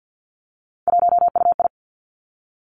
Morse code, heard as a keyed tone, about 50 wpm, 700 Hz. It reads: 1UI
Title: Three Challenging Letter-Number Combinations 50wpm